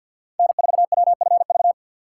Morse code, heard as a keyed tone, about 51 wpm, 700 Hz. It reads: N4WFV
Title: USA Call Signs - 2x Character Spacing - 50wpm